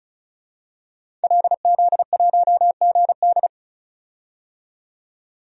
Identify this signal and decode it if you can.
Morse code, heard as a keyed tone, about 34 wpm, 700 Hz. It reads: LZ1GD
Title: DXCC-100 Call Signs - 5 Characters - 35wpm